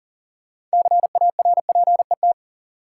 Morse code, heard as a keyed tone, about 40 wpm, 700 Hz. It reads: CARPET
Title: Six Letter Words - 40wpm